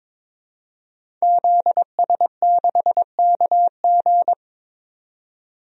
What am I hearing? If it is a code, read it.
Morse code, heard as a keyed tone, about 21 wpm, 700 Hz. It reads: ZS6KG
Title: DXCC-100 Call Signs - 5 Characters - 22wpm